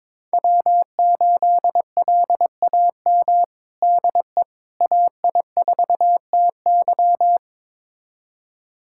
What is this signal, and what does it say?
Morse code, heard as a keyed tone, about 22 wpm, 700 Hz. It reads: W8LAM DE AI4TY
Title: USA Callsign DE Callsign - 5 Characters - 22wpm